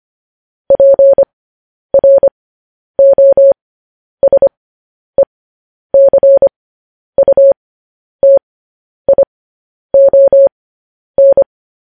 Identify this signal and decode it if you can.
Morse code, heard as a keyed tone, about 25 wpm, spaced as for 10 wpm, 550 Hz. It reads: PROSECUTION